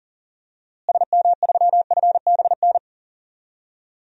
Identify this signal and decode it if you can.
Morse code, heard as a keyed tone, about 40 wpm, 700 Hz. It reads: SM3FBN
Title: DXCC-100 Call Signs - 6 Characters - Repeated 2x - 40wpm